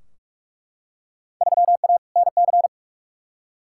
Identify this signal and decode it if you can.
Morse code, heard as a keyed tone, about 42 wpm, 700 Hz. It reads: 3A NC
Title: Field Day Exchanges - 45wpm